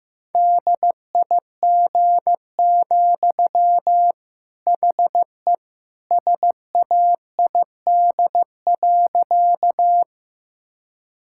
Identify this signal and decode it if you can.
Morse code, heard as a keyed tone, about 15 wpm, 700 Hz. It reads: DIG, HE SAID.
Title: Sentences from Top 1600 Words 15wpm